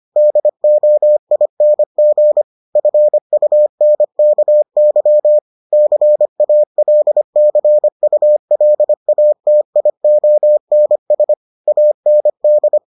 Morse code, heard as a keyed tone, about 25 wpm, 600 Hz. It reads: DOING FUNKY CALCULATIONS AND